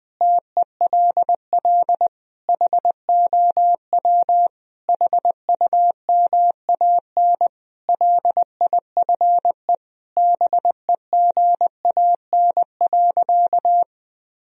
Morse code, heard as a keyed tone, about 21 wpm, 700 Hz. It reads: TELL HOW HUMAN LIFE BEGAN.
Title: Sentences from Top 1000 Words - Review Part 3 - 20wpm